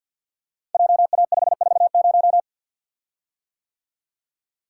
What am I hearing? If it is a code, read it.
Morse code, heard as a keyed tone, about 52 wpm, 690 Hz. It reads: JA540
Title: Code Groups - 5 Letter:Numbers - 50wpm